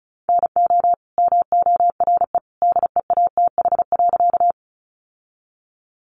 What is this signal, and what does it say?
Morse code, heard as a keyed tone, about 34 wpm, 700 Hz. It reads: NO MORE DEATH.
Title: Sentences from Top 1100 Words 35wpm